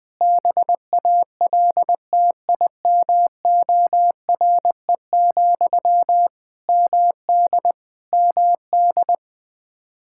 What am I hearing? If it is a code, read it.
Morse code, heard as a keyed tone, about 20 wpm, 700 Hz. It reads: BALTIMORE, MD MD
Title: K1USN SST QSO Practice 20wpm